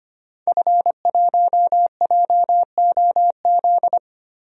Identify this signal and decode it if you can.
Morse code, heard as a keyed tone, about 25 wpm, 700 Hz. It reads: F1JOZ
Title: DXCC-100 Call Signs - 5 Characters - Repeated 3x - 25wpm